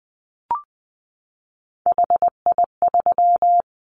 Morse code, heard as a keyed tone, about 20 wpm, 700 Hz. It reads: HI3
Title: Callsign Prefixes for Top 100 DXCC - 20wpm